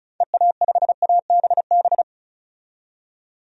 Morse code, heard as a keyed tone, about 36 wpm, 700 Hz. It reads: EA5ABB
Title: DXCC-100 Call Signs - 6 Characters - 35wpm